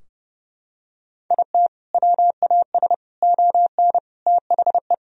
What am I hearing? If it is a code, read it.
Morse code, heard as a keyed tone, about 32 wpm, 700 Hz. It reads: IT WAS ON THE
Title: Sentences from Top 1000 Words - Review Part 2 - 30wpm